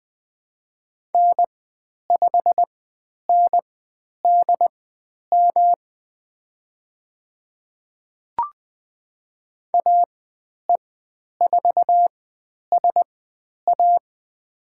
Morse code, heard as a keyed tone, about 20 wpm, spaced as for 10 wpm, 700 Hz. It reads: N5NDM AE4SA
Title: 109 Lesson 18g - Letter m - Call Signs 20wpm